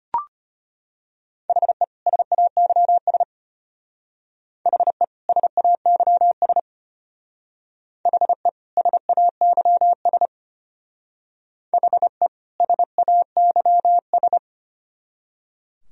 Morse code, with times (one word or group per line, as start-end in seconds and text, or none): 1.5-1.8 HE
2.1-3.2 SAYS
4.7-5.0 HE
5.3-6.6 SAYS
8.1-8.5 HE
8.8-10.3 SAYS
11.7-12.3 HE
12.6-14.4 SAYS